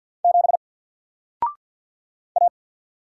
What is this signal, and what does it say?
Morse code, heard as a keyed tone, about 52 wpm, 700 Hz. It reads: 7 A
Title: Single Character (Letter:Number:Punctuation:Prosign) Alt 50wpm